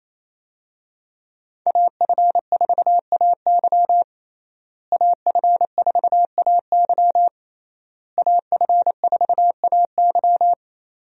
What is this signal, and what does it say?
Morse code, heard as a keyed tone, about 28 wpm, 700 Hz. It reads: AF4AY AF4AY AF4AY
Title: USA Call Signs - 5 Characters - Repeated 3x - 28wpm